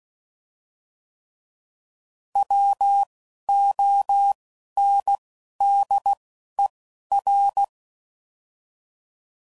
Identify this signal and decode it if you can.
Morse code, heard as a keyed tone, about 16 wpm, 800 Hz. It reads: W O N D E R
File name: Prefixes / By First Letter / Ws